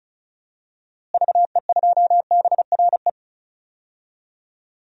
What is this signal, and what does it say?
Morse code, heard as a keyed tone, about 36 wpm, 700 Hz. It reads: VE2BRE
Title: DXCC-100 Call Signs - 6 Characters - Repeated 3x - 35wpm